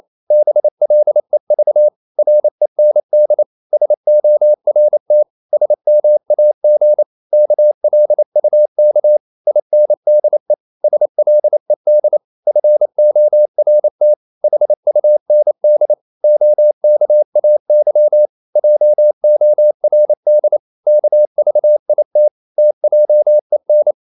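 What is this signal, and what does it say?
Morse code, heard as a keyed tone, about 28 wpm, 600 Hz. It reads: BLEV REND SORT SMAG KLUK INDE SLED FORT HUND OKAY JORD KVIT TJEN
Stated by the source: Danish short words sent in Morse at 28 wpm, from CW med Gnister.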